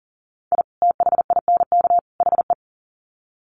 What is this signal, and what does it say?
Morse code, heard as a keyed tone, about 39 wpm, 700 Hz. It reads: I THINK HE
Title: Sets of 3 Words 40wpm